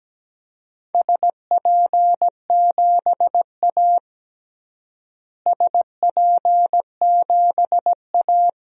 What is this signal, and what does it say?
Morse code, heard as a keyed tone, about 17 wpm, 700 Hz. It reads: SP7A SP7A
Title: DXCC-100 Call Signs - 4 Characters - Repeated 3x - 17wpm